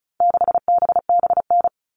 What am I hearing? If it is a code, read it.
Morse code, heard as a keyed tone, about 34 wpm, 700 Hz. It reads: 6BBN
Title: All Cut Numbers - 4-Digit 35wpm